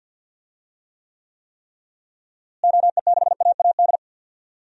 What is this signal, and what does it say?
Morse code, heard as a keyed tone, about 51 wpm, 700 Hz. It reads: OE6AAD
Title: DXCC-100 Call Signs - 6 Characters - Repeated 3x - 50wpm